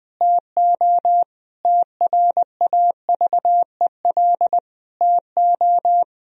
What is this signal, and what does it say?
Morse code, heard as a keyed tone, about 20 wpm, 700 Hz. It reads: TO TRAVEL TO
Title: Sentences from Top 1000 Words - Review Part 3 - 20wpm